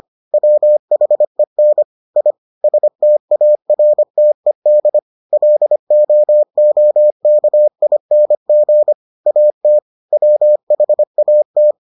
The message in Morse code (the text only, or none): WHEN I STARTED LOOKING AT WHAT